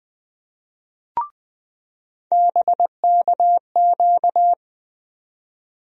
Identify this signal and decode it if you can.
Morse code, heard as a keyed tone, about 20 wpm, 700 Hz. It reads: BKQ
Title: USA Call Sign Postfixes - 3 Letters 20wpm